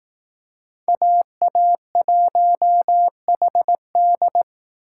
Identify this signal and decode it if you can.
Morse code, heard as a keyed tone, about 18 wpm, 700 Hz. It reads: AA1HD
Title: Speed Racing - USA 2x2 Call Signs 15wpm